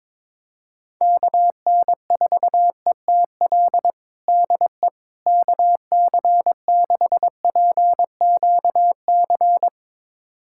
Morse code, heard as a keyed tone, about 21 wpm, 700 Hz. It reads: KN4ETL DE KC6PQC
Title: USA Callsign DE Callsign 22wpm